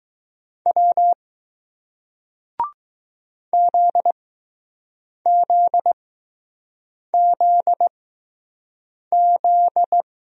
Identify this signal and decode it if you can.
Morse code, heard as a keyed tone, about 20 wpm, spaced as for 6 wpm, 700 Hz. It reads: W ZZZZ